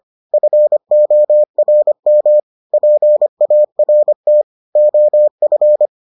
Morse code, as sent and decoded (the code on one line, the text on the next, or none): ..-. --- .-. -- / .--. .- .-. - / --- ..-.
FORM PART OF